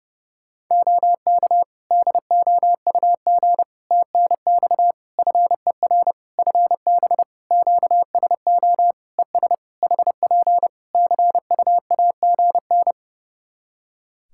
Morse code, heard as a keyed tone, about 30 wpm, 700 Hz. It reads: OK DOUG TNX FER FB QSO ES HP CUAGN
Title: K1USN SST QSO Practice 30wpm